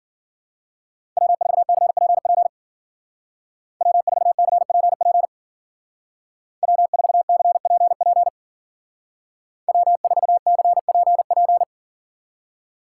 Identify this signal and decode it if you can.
Morse code, heard as a keyed tone, about 49 wpm, 700 Hz. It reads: W4CPP W4CPP W4CPP W4CPP
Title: Speed Racing - USA 1x3 Call Signs 40wpm